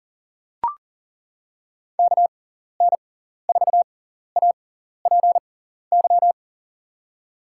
Morse code, heard as a keyed tone, about 37 wpm, spaced as for 14 wpm, 700 Hz. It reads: KN4APY
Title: USA Call Signs - 6 Characters - 6x Character Spacing - 40wpm